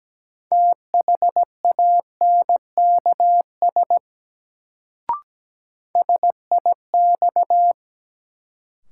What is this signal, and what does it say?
Morse code, heard as a keyed tone, about 17 wpm, 700 Hz. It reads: THANKS SIX
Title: Sentences from Top 600 Words - New Word Drill - 17wpm